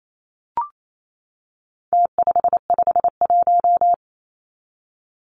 Morse code, heard as a keyed tone, about 28 wpm, 700 Hz. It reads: T551
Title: Numbers - Contest Style - 4-Digit 28wpm